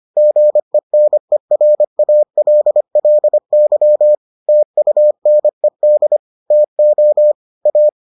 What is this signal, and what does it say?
Morse code, heard as a keyed tone, about 25 wpm, 600 Hz. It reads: GENERALLY TUNED TO A